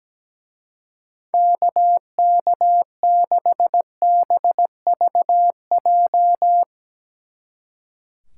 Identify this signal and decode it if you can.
Morse code, heard as a keyed tone, about 17 wpm, 700 Hz. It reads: KK6BVJ
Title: Speed Racing - USA 2x3 Call Signs 17wpm